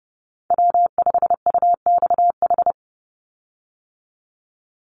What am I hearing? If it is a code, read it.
Morse code, heard as a keyed tone, about 30 wpm, 700 Hz. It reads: W5UXH
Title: Call Signs Repeated 2x - 30wpm